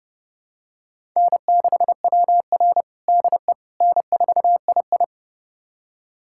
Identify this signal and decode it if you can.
Morse code, heard as a keyed tone, about 30 wpm, 700 Hz. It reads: N6WR DE N4II
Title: USA Callsign DE Callsign - 4 Characters - 30wpm